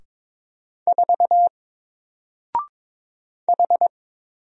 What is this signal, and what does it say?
Morse code, heard as a keyed tone, about 22 wpm, 700 Hz. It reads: 4 H
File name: Single-Letter-Number-22wpm